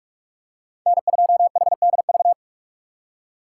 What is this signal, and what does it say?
Morse code, heard as a keyed tone, about 46 wpm, 700 Hz. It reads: N2HDV
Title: USA Call Signs - 5 Characters - 2x Character Spacing - 45wpm